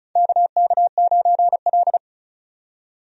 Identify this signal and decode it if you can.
Morse code, heard as a keyed tone, about 36 wpm, 700 Hz. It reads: KK9L
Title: USA Call Signs - 4 Characters - 35wpm